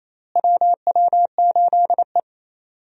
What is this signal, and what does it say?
Morse code, heard as a keyed tone, about 28 wpm, 700 Hz. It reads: WW8E
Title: USA Call Signs - 4 Characters - Repeated 3x - 28wpm